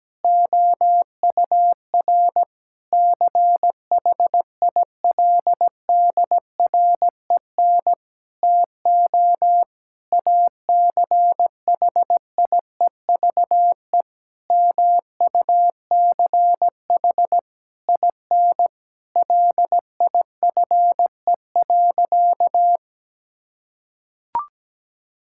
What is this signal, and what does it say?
Morse code, heard as a keyed tone, about 17 wpm, 700 Hz. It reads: OUR CHILDREN TO ACHIEVE MUCH IN LIFE.
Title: Sentences from Top 1000 Words - Review Part 6 - 17wpm